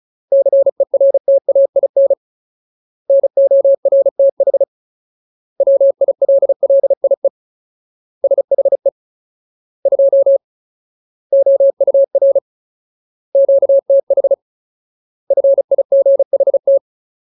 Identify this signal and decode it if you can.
Morse code, heard as a keyed tone, about 34 wpm, 550 Hz. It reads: CERTAIN NORTH WILLIE SHE 2 OUR QTH FIGHT
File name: Ultimate Word List / 35WPM Ultimate